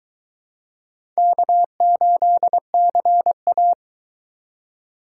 Morse code, heard as a keyed tone, about 21 wpm, 700 Hz. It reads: K8CA